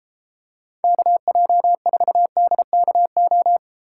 Morse code, heard as a keyed tone, about 33 wpm, 700 Hz. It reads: KJ4DKO